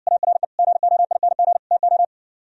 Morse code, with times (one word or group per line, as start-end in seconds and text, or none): 0.1-0.4 ARE
0.6-1.6 GOING
1.7-2.0 TO